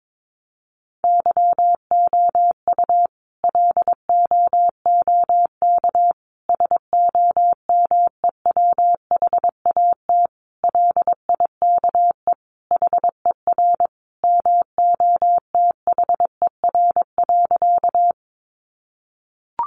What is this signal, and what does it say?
Morse code, heard as a keyed tone, about 22 wpm, 700 Hz. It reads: YOU LOOK SOMEWHAT LIKE HER MOTHER.
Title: Sentences from Top 1300 Words 22wpm